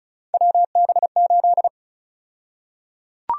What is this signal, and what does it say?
Morse code, heard as a keyed tone, about 36 wpm, 700 Hz. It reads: WB8
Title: USA Call Sign Prefixes - 35wpm